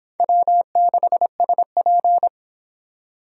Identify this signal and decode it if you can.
Morse code, heard as a keyed tone, about 25 wpm, 700 Hz. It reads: W6SP